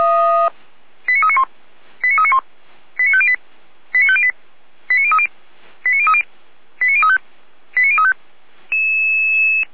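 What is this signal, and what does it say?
Five-tone selective call (ZVEI), heard as quick runs of five tones, tones of about 70 ms. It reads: 89391 89391 89508 89508 89030 89030 89035 89035
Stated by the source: ZVEI five-tone selective call received with an SDR.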